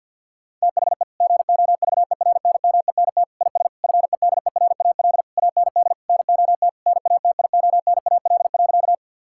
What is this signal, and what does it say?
Morse code, heard as a keyed tone, about 50 wpm, 700 Hz. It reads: THE GOVERNMENT IS FEDERAL AND NOT NATIONAL.
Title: Sentences from Top 1000 Words - Review Part 5 - 50wpm